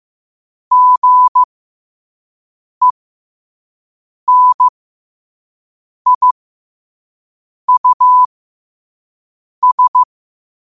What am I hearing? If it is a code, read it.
Morse code, heard as a keyed tone, about 15 wpm, spaced as for 5 wpm, 1000 Hz. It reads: GENIUS
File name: qotdfiles / 5wpm